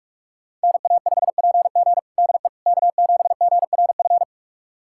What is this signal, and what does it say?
Morse code, heard as a keyed tone, about 44 wpm, 700 Hz. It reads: NA5PG DE K7GRF
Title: USA Callsign DE Callsign - 5 Characters - 45wpm